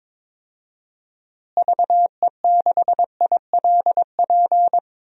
Morse code, heard as a keyed tone, about 21 wpm, 700 Hz. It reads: VE6ILP
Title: DXCC-100 Call Signs - 6 Characters - Repeated 2x - 22wpm